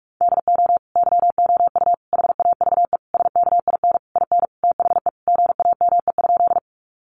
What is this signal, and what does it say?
Morse code, heard as a keyed tone, about 44 wpm, 700 Hz. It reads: DO YOU HAVE SKIN IN THE GAME?
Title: Sentences from Top 1300 Words 45wpm